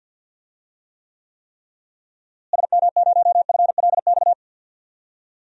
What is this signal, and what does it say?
Morse code, heard as a keyed tone, about 48 wpm, 700 Hz. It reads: SM0FLX
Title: DXCC-100 Call Signs - 6 Characters - Repeated 3x - 50wpm